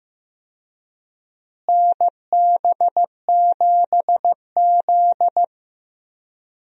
Morse code, heard as a keyed tone, about 15 wpm, 700 Hz. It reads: NB7Z
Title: USA Call Signs - 4 Characters - Repeated 2x - 15wpm